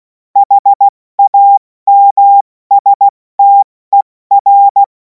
Morse code, heard as a keyed tone, about 16 wpm, spaced as for 14 wpm, 800 Hz. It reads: HAMSTER
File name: Morsekurs / Witze / Witz 10. März 2026 70 F80